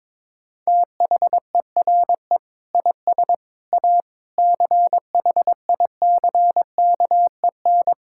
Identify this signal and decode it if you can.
Morse code, heard as a keyed tone, about 22 wpm, 700 Hz. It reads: THERE IS A CHICKEN
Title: Sentences from Top 1000 Words - Review Part 5 - 22wpm